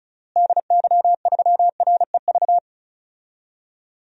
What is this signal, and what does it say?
Morse code, heard as a keyed tone, about 34 wpm, 700 Hz. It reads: DY3REV
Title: DXCC-100 Call Signs - 6 Characters - Repeated 3x - 35wpm